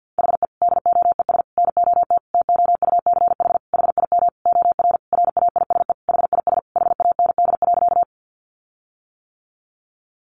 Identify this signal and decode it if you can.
Morse code, heard as a keyed tone, about 48 wpm, 700 Hz. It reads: HE DOES NOT TOUCH HIM OR RAISE HIS HAND.
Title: Sentences from Top 1400 Words 50wpm